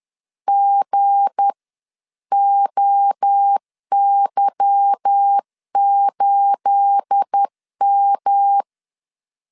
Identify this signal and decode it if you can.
Morse code, heard as a keyed tone, about 10 wpm, spaced as for 9 wpm, 800 Hz. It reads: GOY8M
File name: LEZIONI / 10 SIGNS / 10C DOT 10wpm 2940